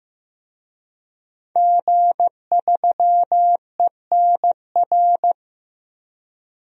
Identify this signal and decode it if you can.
Morse code, heard as a keyed tone, about 15 wpm, 700 Hz. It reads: G3ENR
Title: DXCC-100 Call Signs - 5 Characters - 15wpm